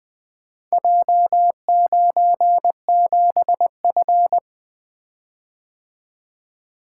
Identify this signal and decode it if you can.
Morse code, heard as a keyed tone, about 20 wpm, 700 Hz. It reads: J97F
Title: Four Letter-Number Combinations - 20wpm